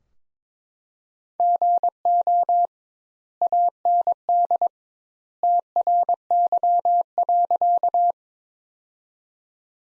Morse code, heard as a keyed tone, about 22 wpm, 700 Hz. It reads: GO AND TRY.
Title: Sentences Top 200 Words - 22wpm